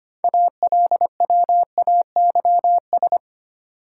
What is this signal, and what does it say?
Morse code, heard as a keyed tone, about 25 wpm, 700 Hz. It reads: ALWAYS
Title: Sentences from Top 300 Words - New Word Drill - 25wpm